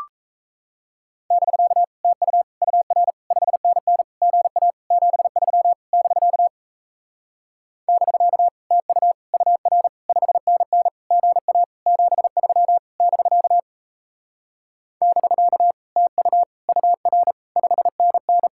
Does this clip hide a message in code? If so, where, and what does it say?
Yes: Morse code from 1.3 to 18.5 s, heard as a keyed tone, about 36 wpm, 700 Hz. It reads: <BK> TU UR 5NN GA 73 <BK> <BK> TU UR 5NN GA 73 <BK> <BK> TU UR 5NN